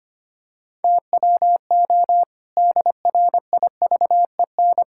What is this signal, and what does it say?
Morse code, heard as a keyed tone, about 25 wpm, 700 Hz. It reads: TWO DRIVEN
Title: Pangram Challenge v2 - 25wpm